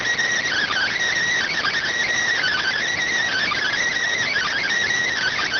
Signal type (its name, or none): Emergency Alert System (SAME) tones